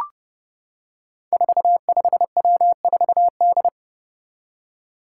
Morse code, heard as a keyed tone, about 30 wpm, 700 Hz. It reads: 45W4D
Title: Five Challenging Letter-Number Combinations 30wpm